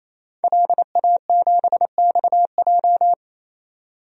Morse code, heard as a keyed tone, about 28 wpm, 700 Hz. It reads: LA7XJ
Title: DXCC-100 Call Signs - 5 Characters - 28wpm